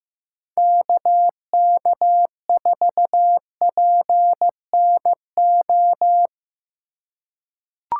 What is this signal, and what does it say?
Morse code, heard as a keyed tone, about 15 wpm, 700 Hz. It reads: KK4PNO
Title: USA Call Signs - 6 Characters - 3x Character Spacing - 15wpm